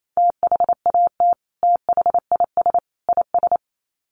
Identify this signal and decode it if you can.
Morse code, heard as a keyed tone, about 28 wpm, 700 Hz. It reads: THAT THIS IS
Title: Sets of 3 Words 28wpm